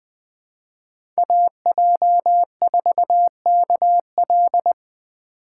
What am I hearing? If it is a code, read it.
Morse code, heard as a keyed tone, about 20 wpm, 700 Hz. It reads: AJ4KL
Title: USA Call Signs - 5 Characters - 4x Character Spacing - 20wpm